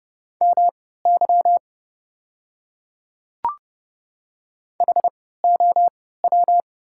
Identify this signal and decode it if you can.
Morse code, heard as a keyed tone, about 30 wpm, spaced as for 17 wpm, 700 Hz. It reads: MY HOW
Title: 3x Character Spacing - Top 100 Words - 30wpm